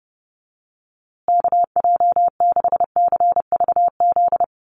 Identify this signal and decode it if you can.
Morse code, heard as a keyed tone, about 30 wpm, 700 Hz. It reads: KJ6CVZ
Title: USA Call Signs Repeated 2x 30wpm